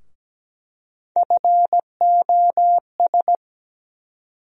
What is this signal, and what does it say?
Morse code, heard as a keyed tone, about 17 wpm, 700 Hz. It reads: FOS
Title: USA Call Sign Postfixes - 3 Letters 17wpm